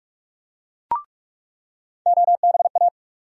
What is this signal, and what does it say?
Morse code, heard as a keyed tone, about 46 wpm, 700 Hz. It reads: OBA
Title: USA Call Sign Postfixes - 3 Letters 45wpm